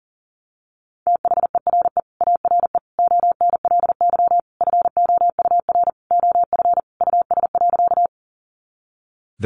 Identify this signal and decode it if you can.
Morse code, heard as a keyed tone, about 40 wpm, 700 Hz. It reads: THERE ARE ONLY FOUR OF US.
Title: Sentences from Top 300 Words 40wpm